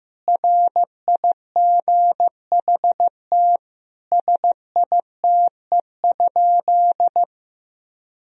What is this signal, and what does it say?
Morse code, heard as a keyed tone, about 16 wpm, 700 Hz. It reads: RIGHT SITE?
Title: Sentences from Top 1000 Words - Review Part 4 - 15wpm